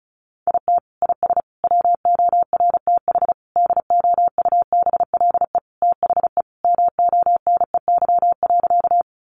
Morse code, heard as a keyed tone, about 34 wpm, 700 Hz. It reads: IT IS WORTH DOUBLE THE MONEY.
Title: Sentences from Top 1400 Words 35wpm